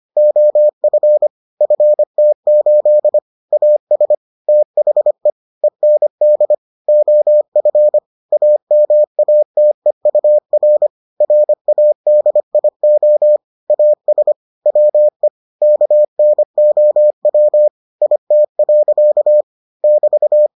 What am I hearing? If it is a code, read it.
Morse code, heard as a keyed tone, about 25 wpm, 600 Hz. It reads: OF FT8 AS THE END OF AMATEUR RADIO AS WE KNOW IT. <BT>